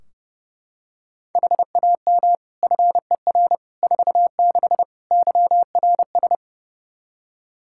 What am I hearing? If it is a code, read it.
Morse code, heard as a keyed tone, about 30 wpm, 700 Hz. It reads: HAM FER 46 YRS
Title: Practice QSO - Exchange 3 - More Info - 30wpm